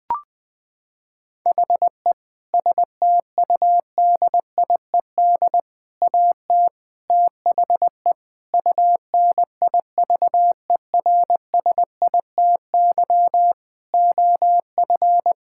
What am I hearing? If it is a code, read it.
Morse code, heard as a keyed tone, about 20 wpm, 700 Hz. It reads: HE STUDIED AT THE UNIVERSITY OF
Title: Sentences with 4-Word Phrases - 20wpm